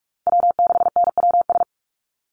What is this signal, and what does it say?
Morse code, heard as a keyed tone, about 44 wpm, 700 Hz. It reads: W6NWS
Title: Call Signs Repeated 2x - 45wpm